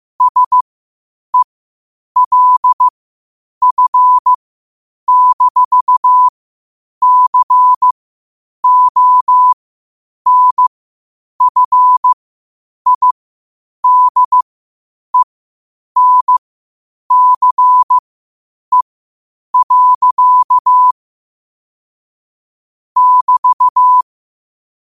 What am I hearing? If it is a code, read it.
Morse code, heard as a keyed tone, about 15 wpm, spaced as for 8 wpm, 1000 Hz. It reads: SELF-CONFIDENCE. <BT>